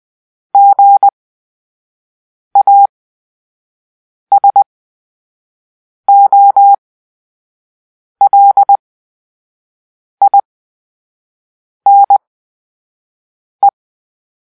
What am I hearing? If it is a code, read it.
Morse code, heard as a keyed tone, about 20 wpm, spaced as for 5 wpm, 800 Hz. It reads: GASOLINE
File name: Grp-3 Drill-2